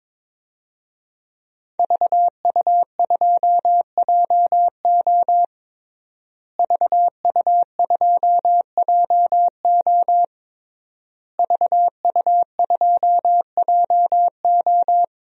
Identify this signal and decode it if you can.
Morse code, heard as a keyed tone, about 21 wpm, 700 Hz. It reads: VU2JO VU2JO VU2JO
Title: DXCC-100 Call Signs - 5 Characters - Repeated 3x - 22wpm